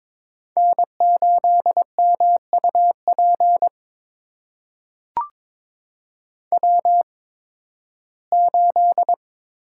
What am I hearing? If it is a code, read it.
Morse code, heard as a keyed tone, about 21 wpm, spaced as for 14 wpm, 700 Hz. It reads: N8MUP W8
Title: USA Call Signs - 5 Characters - 8x Character Spacing - 22wpm